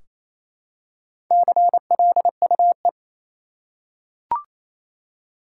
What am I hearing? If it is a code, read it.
Morse code, heard as a keyed tone, about 27 wpm, 700 Hz. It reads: CLUE
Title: Four Letter Words - 28wpm